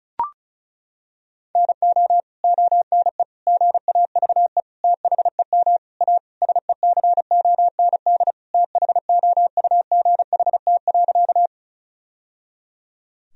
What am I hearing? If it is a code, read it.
Morse code, heard as a keyed tone, about 34 wpm, 700 Hz. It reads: NO ONE GAVE THEM A SECOND THOUGHT.
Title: Sentences from Top 1000 Words - Review Part 1 - 35wpm